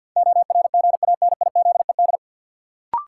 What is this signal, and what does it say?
Morse code, heard as a keyed tone, about 51 wpm, 700 Hz. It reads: ORGANIZED
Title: Sentences from Top 1300 Words - New Word Drill Alt - 50wpm